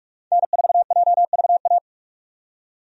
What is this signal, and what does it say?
Morse code, heard as a keyed tone, about 47 wpm, 700 Hz. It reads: N4JVA
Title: USA Call Signs - 5 Characters - 45wpm